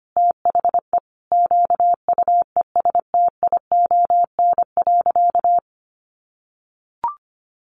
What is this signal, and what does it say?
Morse code, heard as a keyed tone, about 25 wpm, 700 Hz. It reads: THE QUESTION.
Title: Sentences from Top 1400 Words 25wpm